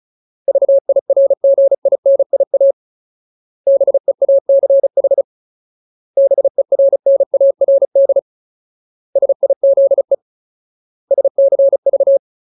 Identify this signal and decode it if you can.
Morse code, heard as a keyed tone, about 34 wpm, 550 Hz. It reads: VIRGINIA BEACH BERNARD SIZE SCV
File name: Ultimate Word List / 35WPM Ultimate